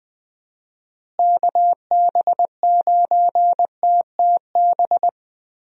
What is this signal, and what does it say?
Morse code, heard as a keyed tone, about 21 wpm, 700 Hz. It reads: KB9TTB
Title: USA Call Signs - 6 Characters - Repeated 2x - 20wpm